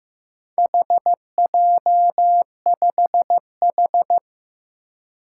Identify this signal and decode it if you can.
Morse code, heard as a keyed tone, about 15 wpm, 700 Hz. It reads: HJ5H
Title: Four Challenging Letter-Number Combinations 15wpm